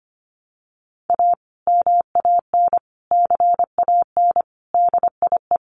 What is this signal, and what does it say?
Morse code, heard as a keyed tone, about 25 wpm, 700 Hz. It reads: A MAN CAN DIE
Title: Sentences from Top 1300 Words 25wpm